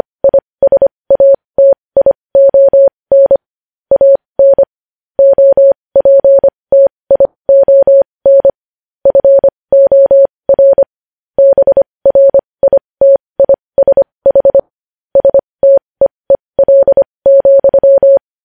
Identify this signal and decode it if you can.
Morse code, heard as a keyed tone, about 25 wpm, spaced as for 20 wpm, 550 Hz. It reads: ISATION AN OPTION FOR BRITISH STEEL,
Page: news headlines in morse code at 20 wpm